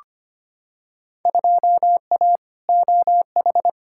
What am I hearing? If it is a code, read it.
Morse code, heard as a keyed tone, about 25 wpm, 700 Hz. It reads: 2A OH
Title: Field Day Exchanges - 25wpm